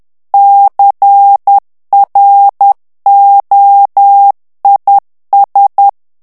Morse code, heard as a keyed tone, about 10 wpm, 800 Hz. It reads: CROIS
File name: FRANCAIS TEXTE CLAIR 2 VITESSE 10 MOTS MINUTES